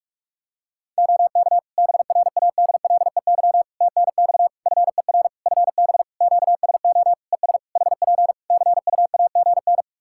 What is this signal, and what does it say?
Morse code, heard as a keyed tone, about 41 wpm, 700 Hz. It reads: OK BRADLEY TNX FER FB QSO ES HP CUAGN
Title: Practice QSO - Exchange 4 - Ending - 45wpm